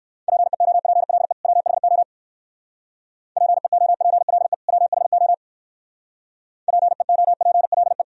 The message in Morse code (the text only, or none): PEOPLE WHO PEOPLE WHO PEOPLE